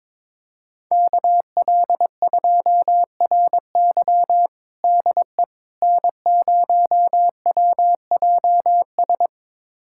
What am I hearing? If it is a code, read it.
Morse code, heard as a keyed tone, about 22 wpm, 700 Hz. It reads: KL2RY DE N0WJS
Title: USA Callsign DE Callsign - 5 Characters - 22wpm